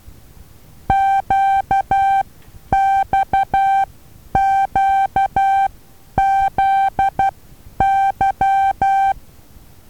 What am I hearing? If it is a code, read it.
Morse code, heard as a keyed tone, about 12 wpm, spaced as for 9 wpm, 780 Hz. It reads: QXQZY
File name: Taviro HA6VA B